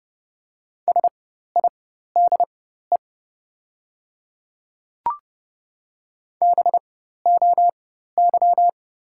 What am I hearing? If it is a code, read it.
Morse code, heard as a keyed tone, about 30 wpm, spaced as for 14 wpm, 700 Hz. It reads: SIDE BOY
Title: 4x Character Spacing - Top 200 Words - 30wpm